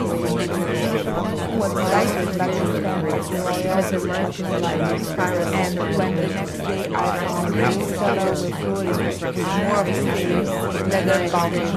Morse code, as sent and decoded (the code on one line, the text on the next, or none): none